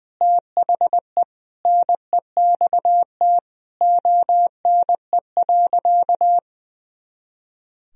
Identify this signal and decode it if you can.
Morse code, heard as a keyed tone, about 20 wpm, 700 Hz. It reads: THE NEXT ONE.
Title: Sentences from Top 400 Words and Abbreviations 20wpm